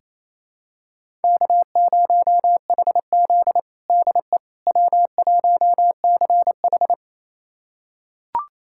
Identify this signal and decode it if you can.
Morse code, heard as a keyed tone, about 28 wpm, 700 Hz. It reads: K0HZ DE W1CH
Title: USA Callsign DE Callsign - 4 Characters - 28wpm